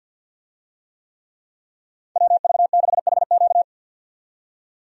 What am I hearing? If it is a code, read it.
Morse code, heard as a keyed tone, about 51 wpm, 700 Hz. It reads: WV6HQ
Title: USA Call Signs - 5 Characters - 50wpm